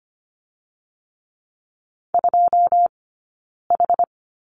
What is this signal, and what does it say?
Morse code, heard as a keyed tone, about 25 wpm, 700 Hz. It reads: 2 H
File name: Lesson 17d - Letter u - Two Characters 25wpm